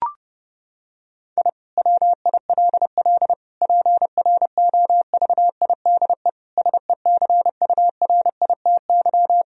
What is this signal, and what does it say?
Morse code, heard as a keyed tone, about 32 wpm, 700 Hz. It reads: I WILL PROVIDE SECURITY
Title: Sentences from Top 1000 Words - Review Part 4 - 30wpm